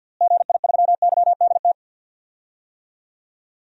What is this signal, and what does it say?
Morse code, heard as a keyed tone, about 51 wpm, 700 Hz. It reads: GI3YDT
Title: DXCC-100 Call Signs - 6 Characters - Repeated 2x - 50wpm